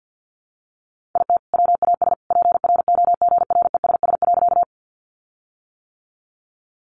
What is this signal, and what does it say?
Morse code, heard as a keyed tone, about 48 wpm, 700 Hz. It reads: IT WAS PROGRESS.
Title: Sentences from Top 1400 Words 50wpm